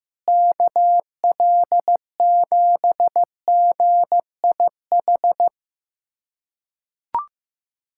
Morse code, heard as a keyed tone, about 16 wpm, 700 Hz. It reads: KL7GIH
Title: USA Callsign DE Callsign - 6 Characters - 15wpm